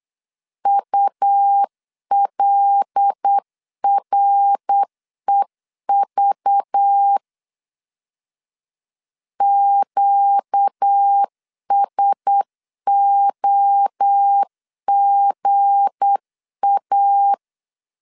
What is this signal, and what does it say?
Morse code, heard as a keyed tone, about 8 wpm, 800 Hz. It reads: ULREV QSOGA
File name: LEZIONI / 7 GF9Z / 7 G 8wpm 3118